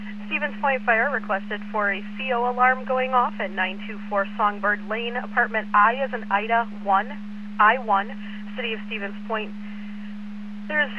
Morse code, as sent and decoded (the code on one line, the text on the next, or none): none